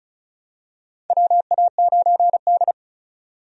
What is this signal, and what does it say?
Morse code, heard as a keyed tone, about 36 wpm, 700 Hz. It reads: WA9D
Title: USA Call Signs - 4 Characters - 35wpm